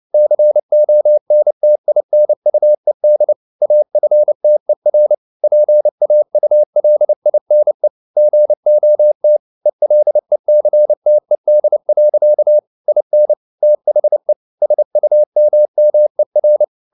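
Morse code, heard as a keyed tone, about 29 wpm, 600 Hz. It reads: CONTINUED AFTER PAULINE GOT ELECTED. IN THE SUMMER
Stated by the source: Morse code at 29 wpm, sending Ken Follett's novel Never in English, from CW med Gnister.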